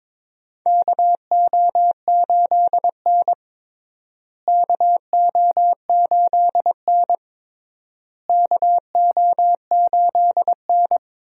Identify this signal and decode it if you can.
Morse code, heard as a keyed tone, about 22 wpm, 700 Hz. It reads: KO8N KO8N KO8N
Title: USA Call Signs - 4 Characters - Repeated 3x - 22wpm